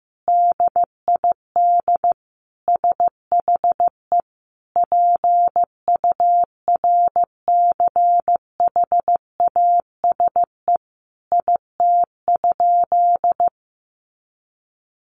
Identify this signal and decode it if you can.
Morse code, heard as a keyed tone, about 15 wpm, 700 Hz. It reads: DID SHE PURCHASE IT?
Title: Sentences from Top 800 Words 15wpm